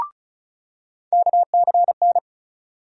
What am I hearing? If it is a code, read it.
Morse code, heard as a keyed tone, about 36 wpm, 700 Hz. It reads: KCN
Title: USA Call Sign Postfixes - 3 Letters 35wpm